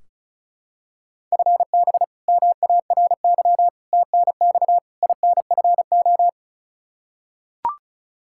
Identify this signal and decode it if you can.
Morse code, heard as a keyed tone, about 34 wpm, 700 Hz. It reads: FB MARY TNX INFO
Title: Practice QSO - Exchange 3 - More Info - 35wpm